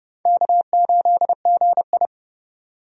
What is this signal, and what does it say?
Morse code, heard as a keyed tone, about 30 wpm, 700 Hz. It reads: K8GI
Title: USA Call Signs - 4 Characters - 30wpm